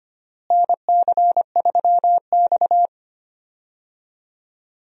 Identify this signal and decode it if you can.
Morse code, heard as a keyed tone, about 25 wpm, 700 Hz. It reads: NC3X
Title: USA Call Signs - 4 Characters - Repeated 2x - 25wpm